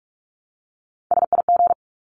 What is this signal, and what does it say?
Morse code, heard as a keyed tone, about 44 wpm, 700 Hz. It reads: SIG
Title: Abbreviations Repeated 3x - 45wpm